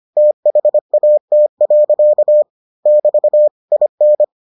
Morse code, heard as a keyed tone, about 25 wpm, 600 Hz. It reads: THAT. <BT> IN